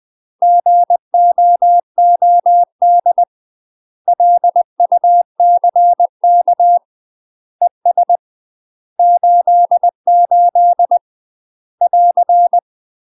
Morse code, heard as a keyed tone, about 20 wpm, 700 Hz. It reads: GOOD LUCK ES 88 <AR>
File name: UK MASTER QSOs TRAINING / 20WPM / UK MASTER QSOs 0063